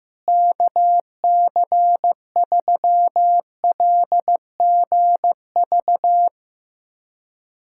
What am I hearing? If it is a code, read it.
Morse code, heard as a keyed tone, about 15 wpm, 700 Hz. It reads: KC3LGV
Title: USA Callsign DE Callsign - 6 Characters - 15wpm